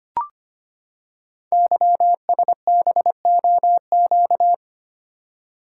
Morse code, heard as a keyed tone, about 25 wpm, 700 Hz. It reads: YSBOQ
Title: Code Groups - 5 Letters - 25wpm